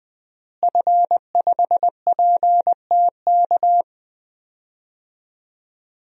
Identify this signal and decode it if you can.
Morse code, heard as a keyed tone, about 20 wpm, 700 Hz. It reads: F5PTK
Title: DXCC-100 Call Signs - 5 Characters - Repeated 2x - 20wpm